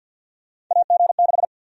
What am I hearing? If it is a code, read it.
Morse code, heard as a keyed tone, about 53 wpm, 700 Hz. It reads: AG6
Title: USA Call Sign Prefixes - Repeated 2x - 50wpm